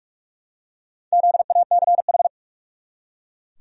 Morse code, heard as a keyed tone, about 43 wpm, 700 Hz. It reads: ZACH
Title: USA Common Names - Max 4 Letters - 45wpm